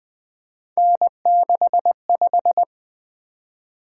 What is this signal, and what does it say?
Morse code, heard as a keyed tone, about 20 wpm, 700 Hz. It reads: N65
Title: Three Letter-Number Combinations - 20wpm